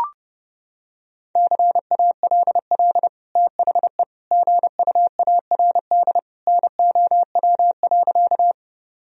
Morse code, heard as a keyed tone, about 30 wpm, 700 Hz. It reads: CALL THE GUARD NOW.
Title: Sentences from Top 1000 Words - Review Part 4 - 30wpm